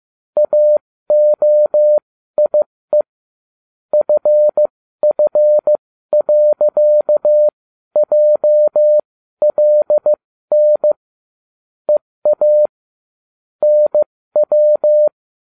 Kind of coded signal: Morse code